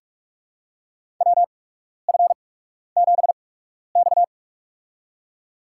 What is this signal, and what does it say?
Morse code, heard as a keyed tone, about 43 wpm, spaced as for 12 wpm, 700 Hz. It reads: WF7X
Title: USA Call Signs - 4 Characters - 8x Character Spacing - 45wpm